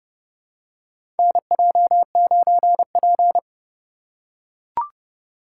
Morse code, heard as a keyed tone, about 30 wpm, 700 Hz. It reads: NJ9P
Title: USA Call Signs - 4 Characters - 4x Character Spacing - 30wpm